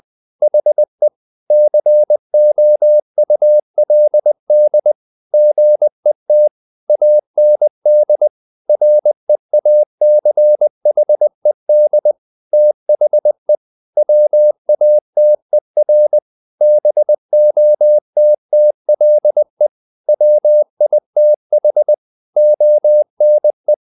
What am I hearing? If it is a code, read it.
Morse code, heard as a keyed tone, about 20 wpm, 600 Hz. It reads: HE COULD GET AND REACHED THE WATER BOTTLE WITH ONE
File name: the old man and the sea / the old man and the sea part 1 20wpm